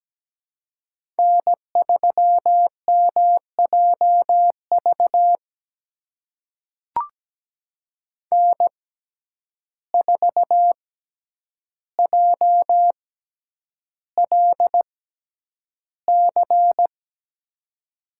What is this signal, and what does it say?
Morse code, heard as a keyed tone, about 17 wpm, spaced as for 9 wpm, 700 Hz. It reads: N3MJV N4JLC
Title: USA Call Signs - 5 Characters - 6x Character Spacing - 17wpm